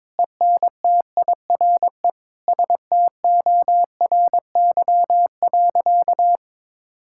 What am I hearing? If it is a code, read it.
Morse code, heard as a keyed tone, about 22 wpm, 700 Hz. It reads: ENTIRE STORY.
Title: Sentences from Top 1000 Words - Review Part 5 - 22wpm